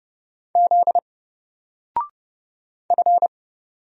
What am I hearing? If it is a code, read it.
Morse code, heard as a keyed tone, about 31 wpm, 700 Hz. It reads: Z F
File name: Single-Letter-Number-30wpm